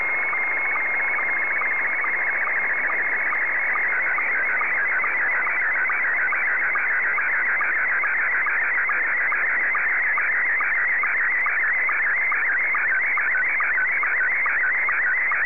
SSTV (slow-scan television), Scottie 1.
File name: sstv / S1-VK3DFL